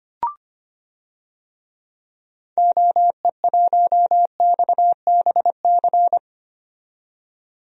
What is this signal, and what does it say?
Morse code, heard as a keyed tone, about 25 wpm, 700 Hz. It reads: OE1XBC
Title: DXCC-100 Call Signs - 6 Characters - 25wpm